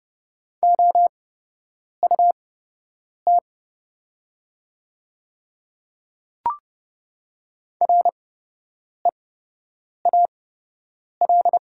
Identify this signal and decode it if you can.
Morse code, heard as a keyed tone, about 30 wpm, spaced as for 8 wpm, 700 Hz. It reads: OUT REAL
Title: 8x Character Spacing - Top 300 Words - 30wpm